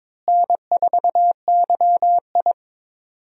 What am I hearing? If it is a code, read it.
Morse code, heard as a keyed tone, about 22 wpm, 700 Hz. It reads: N4YI
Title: USA Call Signs - 4 Characters - Repeated 2x - 22wpm